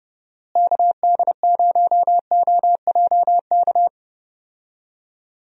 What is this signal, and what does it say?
Morse code, heard as a keyed tone, about 32 wpm, 700 Hz. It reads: KD0OJK
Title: USA Call Signs - 6 Characters - Repeated 3x - 30wpm